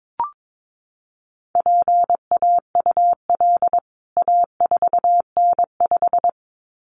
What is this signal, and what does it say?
Morse code, heard as a keyed tone, about 22 wpm, 700 Hz. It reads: PAUL A4N5
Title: CWT Exchange 22wpm